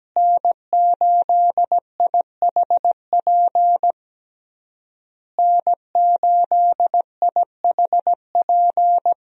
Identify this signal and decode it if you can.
Morse code, heard as a keyed tone, about 17 wpm, 700 Hz. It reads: N8IHP N8IHP
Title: USA Call Signs - 5 Characters - Repeated 3x - 17wpm